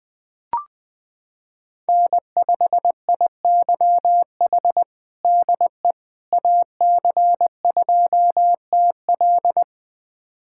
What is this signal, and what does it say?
Morse code, heard as a keyed tone, about 20 wpm, 700 Hz. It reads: N5IYH DE AC2TL
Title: USA Callsign DE Callsign - 5 Characters - 20wpm